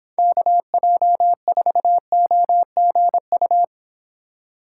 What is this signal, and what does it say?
Morse code, heard as a keyed tone, about 26 wpm, 700 Hz. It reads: KJ4OGU